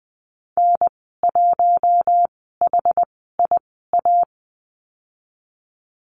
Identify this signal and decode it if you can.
Morse code, heard as a keyed tone, about 20 wpm, spaced as for 14 wpm, 700 Hz. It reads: N1HIA
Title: 069 Lesson 12g - Letter d - Call Signs 20wpm